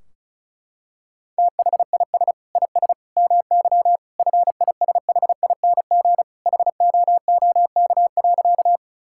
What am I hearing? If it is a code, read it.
Morse code, heard as a keyed tone, about 34 wpm, 700 Hz. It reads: THIS IS MY FISHING HOOK.
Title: Sentences from Top 1600 Words 35wpm